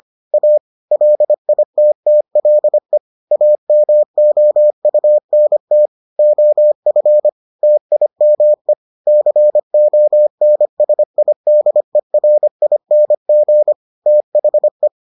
Morse code, heard as a keyed tone, about 25 wpm, 600 Hz. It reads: A LITTLE AMOUNT OF TIME CONSIDERING THE